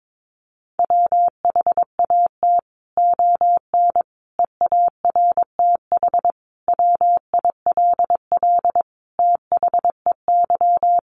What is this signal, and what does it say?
Morse code, heard as a keyed tone, about 22 wpm, 700 Hz. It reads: WHAT ON EARTH WILL THEY
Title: Sentences from Top 1100 Words 22wpm